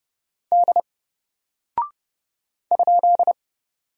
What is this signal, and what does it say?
Morse code, heard as a keyed tone, about 30 wpm, 700 Hz. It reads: D ?
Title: Single Character (Letter:Number:Punctuation:Prosign) Alt 30wpm